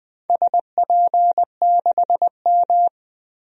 Morse code, heard as a keyed tone, about 21 wpm, 700 Hz. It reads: SP6M
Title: DXCC-100 Call Signs - 4 Characters - Repeated 2x - 20wpm